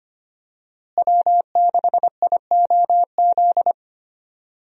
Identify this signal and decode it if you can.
Morse code, heard as a keyed tone, about 25 wpm, 700 Hz. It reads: W6IOZ
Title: USA Call Signs - 5 Characters - Repeated 2x - 25wpm